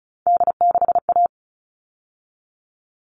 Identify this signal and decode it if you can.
Morse code, heard as a keyed tone, about 34 wpm, 700 Hz. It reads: D6A
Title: All Cut Numbers - 3-Digit 35wpm